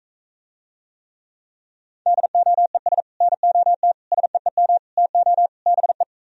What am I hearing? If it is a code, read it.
Morse code, heard as a keyed tone, about 45 wpm, 700 Hz. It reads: DOES NOT SEEM TO BE